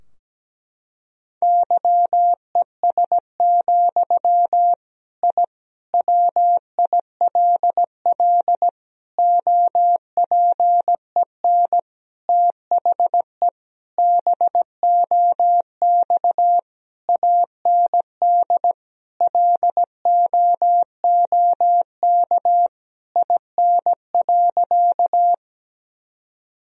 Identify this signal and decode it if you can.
Morse code, heard as a keyed tone, about 17 wpm, 700 Hz. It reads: YES, I WILL OPEN THE BOX AND LOOK IN.
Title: Sentences from Top 1000 Words - Review Part 2 - 17wpm